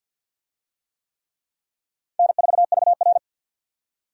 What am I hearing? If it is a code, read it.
Morse code, heard as a keyed tone, about 52 wpm, 700 Hz. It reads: N4VR
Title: USA Call Signs - 4 Characters - 50wpm